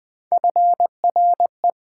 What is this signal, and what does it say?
Morse code, heard as a keyed tone, about 20 wpm, 700 Hz. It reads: FRE
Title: Build A Word - 1 Letter At A Time - 5-Letter Words - 20wpm